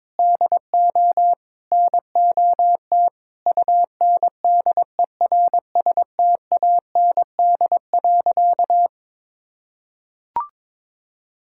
Morse code, heard as a keyed tone, about 22 wpm, 700 Hz. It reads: DO NOT UNDERSTAND.
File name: practice / 22wpm / Sentences with 3-Word Phrases - 22wpm